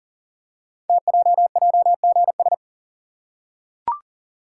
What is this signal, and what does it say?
Morse code, heard as a keyed tone, about 38 wpm, 700 Hz. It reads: TJJGS
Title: Code Groups - 5 Letters - 40wpm